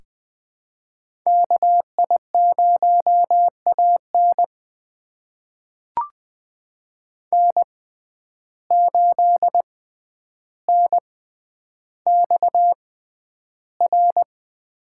Morse code, heard as a keyed tone, about 21 wpm, spaced as for 10 wpm, 700 Hz. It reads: KI0AN N8NXR
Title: USA Call Signs - 5 Characters - 6x Character Spacing - 20wpm